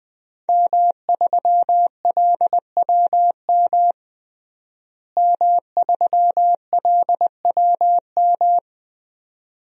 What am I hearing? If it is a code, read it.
Morse code, heard as a keyed tone, about 20 wpm, 700 Hz. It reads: M3LWM M3LWM
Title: DXCC-100 Call Signs - 5 Characters - Repeated 2x - 20wpm